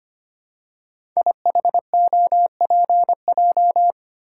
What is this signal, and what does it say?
Morse code, heard as a keyed tone, about 25 wpm, 700 Hz. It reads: IHOPJ
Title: Code Groups - 5 Letters - 25wpm